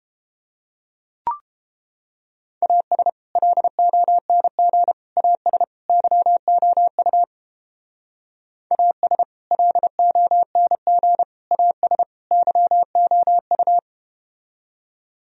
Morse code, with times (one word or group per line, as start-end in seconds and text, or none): 2.6-3.1 AS
3.4-4.9 LONG
5.2-5.6 AS
5.9-7.2 YOU
8.7-9.2 AS
9.5-11.2 LONG
11.5-12.0 AS
12.3-13.8 YOU